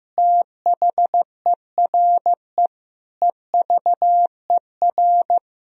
Morse code, heard as a keyed tone, about 16 wpm, 700 Hz. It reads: THERE EVER
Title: Sentences from Top 400 Words and Abbreviations 15wpm